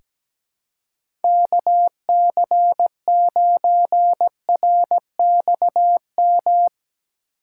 Morse code, heard as a keyed tone, about 17 wpm, 700 Hz. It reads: KC9RXM
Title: USA Call Signs - 6 Characters - 3x Character Spacing - 17wpm